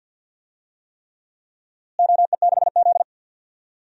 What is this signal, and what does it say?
Morse code, heard as a keyed tone, about 46 wpm, 700 Hz. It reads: OE6Z
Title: DXCC-100 Call Signs - 4 Characters - Repeated 3x - 50wpm